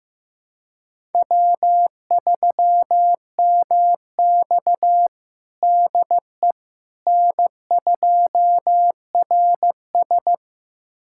Morse code, heard as a keyed tone, about 15 wpm, 700 Hz. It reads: W3MX DE N2RS
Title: USA Callsign DE Callsign - 4 Characters - 15wpm